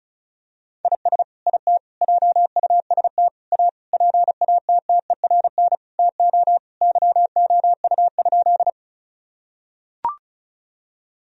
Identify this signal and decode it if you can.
Morse code, heard as a keyed tone, about 34 wpm, 700 Hz. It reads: IS IT JUST A PATTERN TO YOU?
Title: Sentences from Top 1000 Words - Review Part 2 - 35wpm